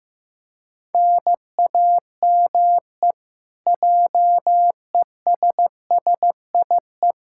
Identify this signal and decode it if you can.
Morse code, heard as a keyed tone, about 16 wpm, 700 Hz. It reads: NAME JESSIE
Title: Practice QSO - Exchange 1 - Beginning - 15wpm